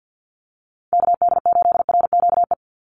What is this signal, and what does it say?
Morse code, heard as a keyed tone, about 50 wpm, 700 Hz. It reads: KD8RQE
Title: Call Signs Repeated 2x - 50wpm